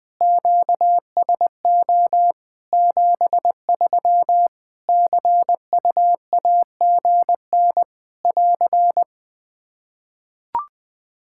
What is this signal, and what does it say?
Morse code, heard as a keyed tone, about 20 wpm, 700 Hz. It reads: QSO 73 CUAGN <AR>
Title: Practice QSO - Exchange 4 - Ending - 20wpm